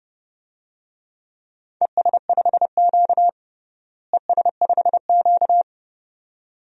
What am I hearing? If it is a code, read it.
Morse code, heard as a keyed tone, about 30 wpm, 700 Hz. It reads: ES5Q ES5Q
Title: DXCC-100 Call Signs - 4 Characters - Repeated 2x - 30wpm